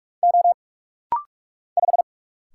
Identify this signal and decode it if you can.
Morse code, heard as a keyed tone, about 46 wpm, 700 Hz. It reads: O 5
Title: Single Letter-Number Alt - Rapid-Fire 45wpm